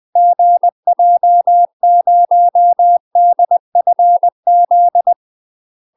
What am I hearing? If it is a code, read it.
Morse code, heard as a keyed tone, about 20 wpm, 700 Hz. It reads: GJ0DFZ
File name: UK MASTER QSOs TRAINING/20WPM